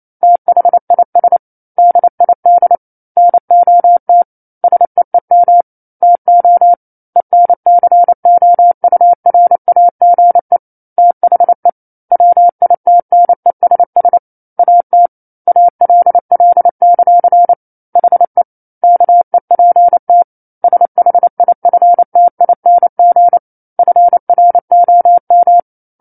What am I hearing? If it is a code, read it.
Morse code, heard as a keyed tone, about 28 wpm, 700 Hz. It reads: THIS DID NOT SEEM TO ENCOURAGE THE WITNESS AT ALL; HE KEPT SHIFTING FROM